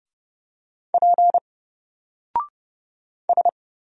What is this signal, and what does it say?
Morse code, heard as a keyed tone, about 29 wpm, 700 Hz. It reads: P S
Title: Single Character (Letter:Number:Punctuation:Prosign) 30wpm